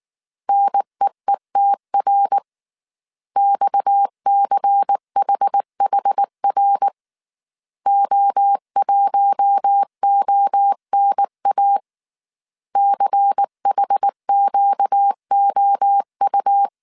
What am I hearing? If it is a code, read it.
Morse code, heard as a keyed tone, about 19 wpm, 800 Hz. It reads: NEETR XCHHR O1ONA CHQOU
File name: CW / DET / 1 6 1500C S A 18wpm 2255